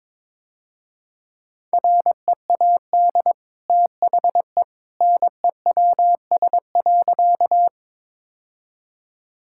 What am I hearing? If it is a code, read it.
Morse code, heard as a keyed tone, about 22 wpm, 690 Hz. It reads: READ THE NEWS.
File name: practice / 22wpm / Easy Sentences with Hints - 22wpm